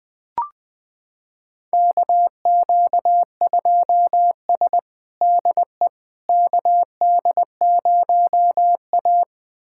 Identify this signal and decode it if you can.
Morse code, heard as a keyed tone, about 20 wpm, 700 Hz. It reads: KQ2S DE KD0A
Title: USA Callsign DE Callsign - 4 Characters - 20wpm